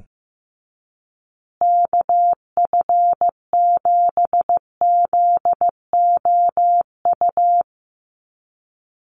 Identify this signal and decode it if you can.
Morse code, heard as a keyed tone, about 15 wpm, 700 Hz. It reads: KF7ZOU
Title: USA Call Signs Repeated 3x 15wpm